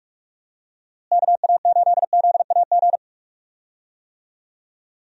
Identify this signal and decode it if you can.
Morse code, heard as a keyed tone, about 48 wpm, 700 Hz. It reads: KA8ZAG
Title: USA Call Signs - 6 Characters - Repeated 2x - 45wpm